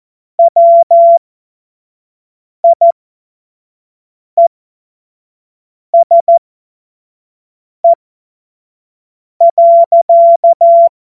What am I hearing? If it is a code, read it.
Morse code, heard as a keyed tone, about 14 wpm, spaced as for 5 wpm, 680 Hz. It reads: WIESE.
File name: Witz-23-September-2025-25-F70